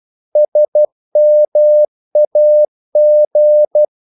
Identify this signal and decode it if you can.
Morse code, heard as a keyed tone, about 12 wpm, 600 Hz. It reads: SMAG
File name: Korte ord DK 12wpm